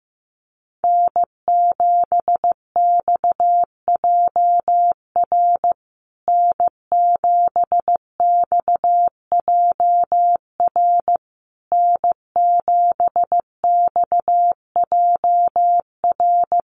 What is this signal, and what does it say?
Morse code, heard as a keyed tone, about 15 wpm, 700 Hz. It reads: N7XJR N7XJR N7XJR
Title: USA Call Signs Repeated 3x 15wpm